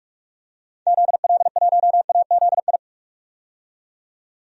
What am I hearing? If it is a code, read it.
Morse code, heard as a keyed tone, about 41 wpm, 690 Hz. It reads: ZL1AZI
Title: DXCC-100 Call Signs - 6 Characters - 45wpm